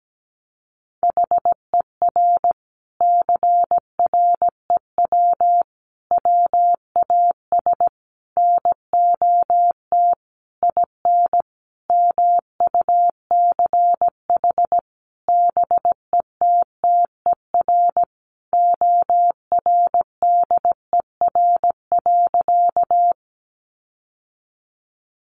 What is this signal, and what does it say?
Morse code, heard as a keyed tone, about 17 wpm, 700 Hz. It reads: HER CREW WAS NOT IN MUCH BETTER ORDER.
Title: Sentences from Top 1500 Words 17wpm